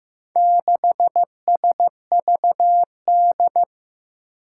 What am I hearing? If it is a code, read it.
Morse code, heard as a keyed tone, about 15 wpm, 700 Hz. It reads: 6SVD
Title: Four Challenging Letter-Number Combinations 15wpm